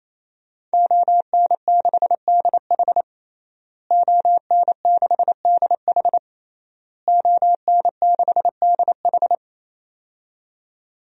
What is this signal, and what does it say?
Morse code, heard as a keyed tone, about 28 wpm, 700 Hz. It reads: ON6DH ON6DH ON6DH
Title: DXCC-100 Call Signs - 5 Characters - Repeated 3x - 28wpm